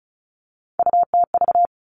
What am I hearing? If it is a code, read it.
Morse code, heard as a keyed tone, about 34 wpm, 700 Hz. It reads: UTV
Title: All Cut Numbers - 3-Digit 35wpm